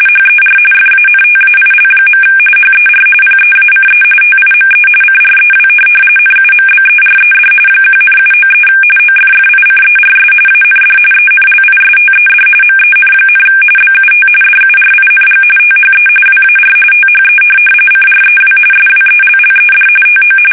RTTY, about 45.45 baud. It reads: HER FANTAIL. HER PEACE TIME CREW NUMBERED 1,793 OFFICERS AND ENLISTED MEN, BUT DURING THE WAR THE CREW SWELLED TO
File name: fsk850 / USS-Massachusetts